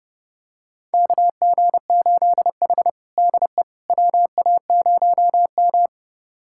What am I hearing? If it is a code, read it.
Morse code, heard as a keyed tone, about 30 wpm, 700 Hz. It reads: KG8H DE WA0M
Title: USA Callsign DE Callsign - 4 Characters - 30wpm